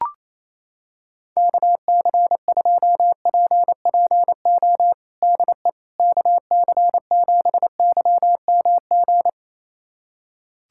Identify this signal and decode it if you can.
Morse code, heard as a keyed tone, about 28 wpm, 700 Hz. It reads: KC2PPO DE KC7YMG
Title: USA Callsign DE Callsign - 6 Characters - 28wpm